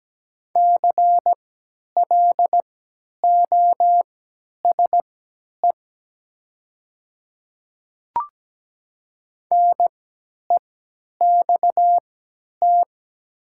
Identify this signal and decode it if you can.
Morse code, heard as a keyed tone, about 17 wpm, spaced as for 10 wpm, 690 Hz. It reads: CLOSE NEXT
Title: 3x Character Spacing - Top 300 Words - 17wpm